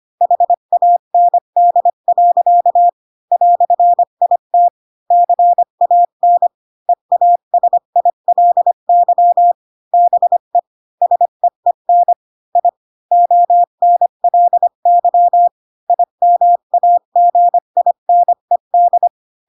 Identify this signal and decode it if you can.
Morse code, heard as a keyed tone, about 25 wpm, 700 Hz. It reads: HAND. "IT CAN EASILY BE SEEN I ONLY IMAGINED
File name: Pinocchio / Pinocchio ch1 20 25 ik1hge